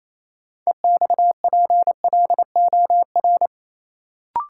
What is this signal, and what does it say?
Morse code, heard as a keyed tone, about 28 wpm, 700 Hz. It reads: EXPLOR
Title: Sentences from Top 1100 Words - New Word Drill - 28wpm